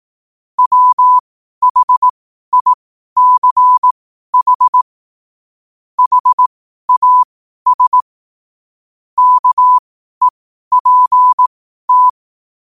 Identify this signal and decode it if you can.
Morse code, heard as a keyed tone, about 18 wpm, spaced as for 12 wpm, 1000 Hz. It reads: WHICH HAS KEPT
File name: qotdfiles / 12wpm